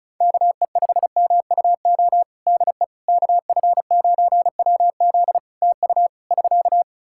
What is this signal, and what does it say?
Morse code, heard as a keyed tone, about 34 wpm, 700 Hz. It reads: KE5MUO DE KF9WZ TU <SK>
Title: K1USN SST QSO Practice 35wpm